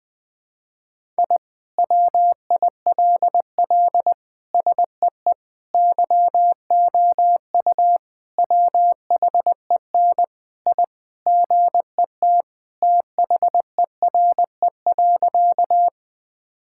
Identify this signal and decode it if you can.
Morse code, heard as a keyed tone, about 20 wpm, 700 Hz. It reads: I WILL SEE YOU WHEN I GET THERE.
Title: Sentences with 2-Word Phrases - 20wpm